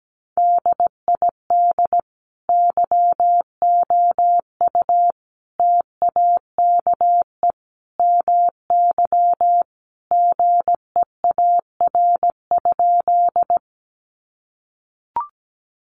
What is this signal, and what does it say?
Morse code, heard as a keyed tone, about 17 wpm, 700 Hz. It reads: DID YOU TAKE MY GEAR?
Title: Sentences from Top 1400 Words 17wpm